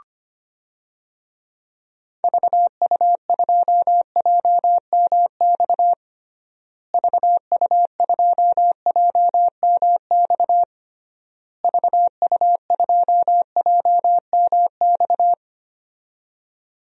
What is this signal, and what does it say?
Morse code, heard as a keyed tone, about 25 wpm, 700 Hz. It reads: VU2JMX VU2JMX VU2JMX
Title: DXCC-100 Call Signs - 6 Characters - Repeated 3x - 25wpm